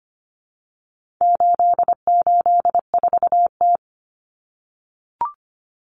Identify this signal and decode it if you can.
Morse code, heard as a keyed tone, about 25 wpm, 700 Hz. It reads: 884T
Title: Common Cut Numbers - 4-Digit 25wpm